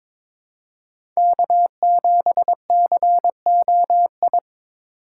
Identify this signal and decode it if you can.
Morse code, heard as a keyed tone, about 22 wpm, 700 Hz. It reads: K7COI
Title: USA Call Signs - 5 Characters - 4x Character Spacing - 22wpm